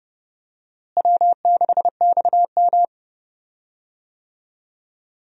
Morse code, heard as a keyed tone, about 30 wpm, 700 Hz. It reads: W6XM
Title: USA Call Signs - 4 Characters - Repeated 2x - 30wpm